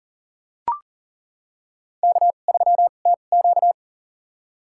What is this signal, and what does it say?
Morse code, heard as a keyed tone, about 40 wpm, spaced as for 29 wpm, 700 Hz. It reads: K3TQ
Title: USA Call Signs - 4 Characters - 2x Character Spacing - 40wpm